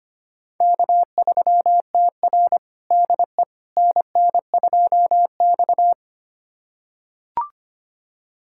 Morse code, heard as a keyed tone, about 25 wpm, 700 Hz. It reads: K3TR DE NN2X
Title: USA Callsign DE Callsign - 4 Characters - 25wpm